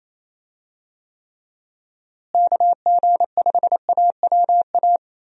Morse code, heard as a keyed tone, about 28 wpm, 700 Hz. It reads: KG5AWA
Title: USA Call Signs - 6 Characters - Repeated 2x - 28wpm